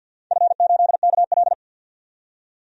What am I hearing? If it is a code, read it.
Morse code, heard as a keyed tone, about 47 wpm, 700 Hz. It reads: F8KL
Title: DXCC-100 Call Signs - 4 Characters - 50wpm